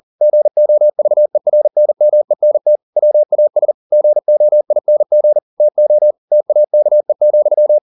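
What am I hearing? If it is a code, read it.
Morse code, heard as a keyed tone, about 40 wpm, 600 Hz. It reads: GOVERNMENT WAS GOING TO TAKE,
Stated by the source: Morse code at 40 wpm, practice text, CW med Gnister.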